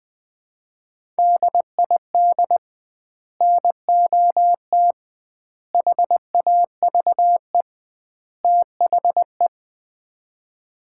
Morse code, heard as a keyed tone, about 20 wpm, 700 Hz. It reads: DID NOT HAVE THE
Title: 2x Word-Spacing - Sets of 4 Words - 20wpm